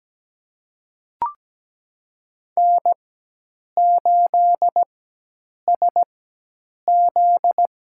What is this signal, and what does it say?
Morse code, heard as a keyed tone, about 17 wpm, spaced as for 8 wpm, 700 Hz. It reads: N8SZ
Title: USA Call Signs - 4 Characters - 4x Character Spacing - 17wpm